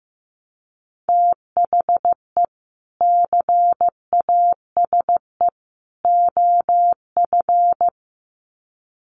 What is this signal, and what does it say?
Morse code, heard as a keyed tone, about 15 wpm, 700 Hz. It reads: THE CASE OF
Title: Sets of 3 Words 15wpm